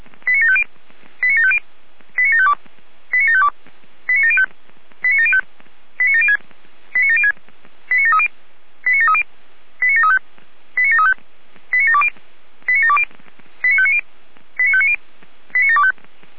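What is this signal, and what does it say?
Five-tone selective call (ZVEI), heard as quick runs of five tones, tones of about 70 ms. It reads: 89750 89750 89752 89752 89705 89705 89706 89706 89730 89730 89735 89735 89720 89720 89590 89590 89725
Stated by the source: ZVEI five-tone selective call received on SDR.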